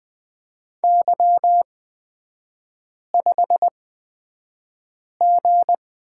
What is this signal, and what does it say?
Morse code, heard as a keyed tone, about 20 wpm, spaced as for 5 wpm, 700 Hz. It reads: Y5G
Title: 236 Lesson 38e - Letter q - Three Characters 20wpm